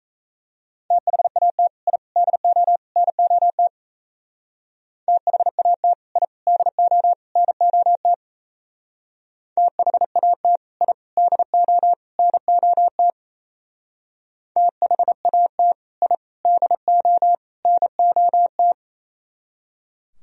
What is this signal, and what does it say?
Morse code, heard as a keyed tone, about 35 wpm, 700 Hz. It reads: THAT I DO NOT THAT I DO NOT THAT I DO NOT THAT I DO NOT